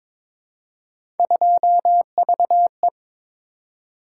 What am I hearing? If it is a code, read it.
Morse code, heard as a keyed tone, about 22 wpm, 700 Hz. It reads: 2VE
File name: practice / 22wpm / Three Letter-Number Combinations - 22wpm